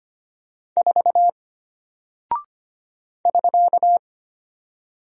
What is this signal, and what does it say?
Morse code, heard as a keyed tone, about 25 wpm, 700 Hz. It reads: 4 <SK>
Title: Single Character (Letter:Number:Punctuation:Prosign) Alt 25wpm